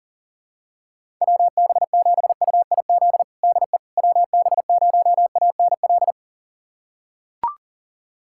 Morse code, heard as a keyed tone, about 40 wpm, 700 Hz. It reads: WB7UIZ DE WB0ANL
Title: USA Callsign DE Callsign - 6 Characters - 40wpm